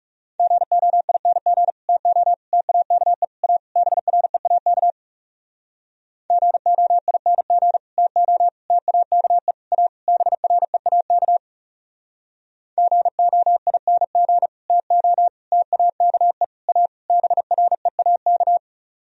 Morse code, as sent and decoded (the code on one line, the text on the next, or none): --. --- .. -. --. / - --- / - .- -.- . / .- / -... .-. .- -.- / --. --- .. -. --. / - --- / - .- -.- . / .- / -... .-. .- -.- / --. --- .. -. --. / - --- / - .- -.- . / .- / -... .-. .- -.-
GOING TO TAKE A BREAK GOING TO TAKE A BREAK GOING TO TAKE A BREAK